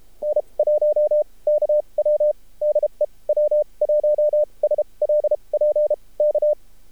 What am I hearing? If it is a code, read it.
Morse code, heard as a keyed tone, about 30 wpm, spaced as for 24 wpm, 600 Hz. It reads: N1KW DEW1SLP K
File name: archived / CRSnet